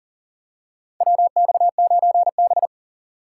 Morse code, heard as a keyed tone, about 40 wpm, 700 Hz. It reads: WX9B
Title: USA Call Signs - 4 Characters - 2x Character Spacing - 40wpm